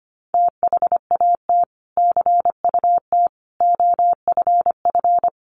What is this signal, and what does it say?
Morse code, heard as a keyed tone, about 25 wpm, 700 Hz. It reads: THAT CUT OFF
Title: Sentences from Top 1100 Words 25wpm